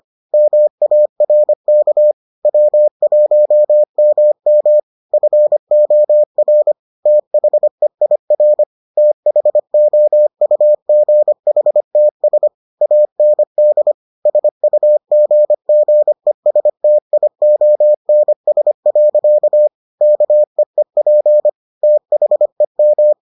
Morse code, heard as a keyed tone, about 25 wpm, 600 Hz. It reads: MARK W1MM FOR THEIR THOUGHTS AND SUGGESTIONS. KEEP THEM